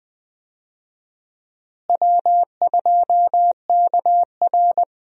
Morse code, heard as a keyed tone, about 20 wpm, 700 Hz. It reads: W2KR
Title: USA Call Signs - 4 Characters - Repeated 2x - 20wpm